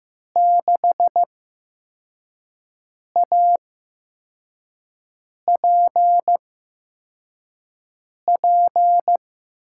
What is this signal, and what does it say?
Morse code, heard as a keyed tone, about 15 wpm, spaced as for 4 wpm, 700 Hz. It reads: 6APP
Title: USA Call Signs - 6 Characters - 8x Character Spacing - 15wpm